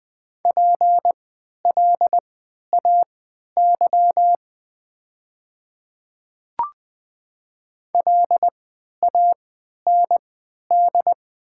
Morse code, heard as a keyed tone, about 20 wpm, spaced as for 11 wpm, 690 Hz. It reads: PLAY LAND
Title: 3x Character Spacing - Top 300 Words - 20wpm